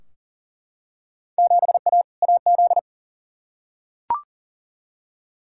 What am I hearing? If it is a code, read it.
Morse code, heard as a keyed tone, about 40 wpm, 700 Hz. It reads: 7A AZ
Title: Field Day Exchanges - 40wpm